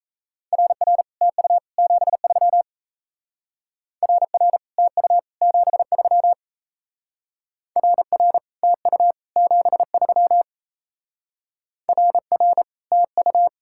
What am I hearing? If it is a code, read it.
Morse code, heard as a keyed tone, about 35 wpm, 700 Hz. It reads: RR TU 73 RR TU 73 RR TU 73 RR TU